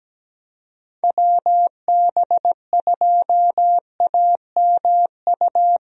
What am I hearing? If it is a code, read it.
Morse code, heard as a keyed tone, about 17 wpm, 700 Hz. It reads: WB2AMU
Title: USA Call Signs - 6 Characters - 17wpm